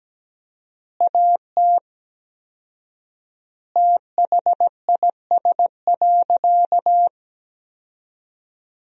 Morse code, heard as a keyed tone, about 17 wpm, 700 Hz. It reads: AT THIS.
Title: 4x Word-Spacing - Sentences from Top 300 Words - 17wpm